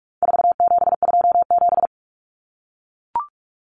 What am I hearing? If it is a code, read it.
Morse code, heard as a keyed tone, about 44 wpm, 700 Hz. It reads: 4727
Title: Numbers - 4-Digit 45wpm